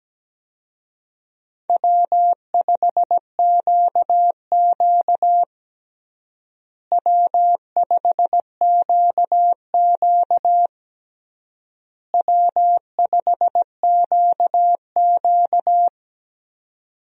Morse code, heard as a keyed tone, about 17 wpm, 700 Hz. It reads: W5QQ W5QQ W5QQ
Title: USA Call Signs - 4 Characters - Repeated 3x - 17wpm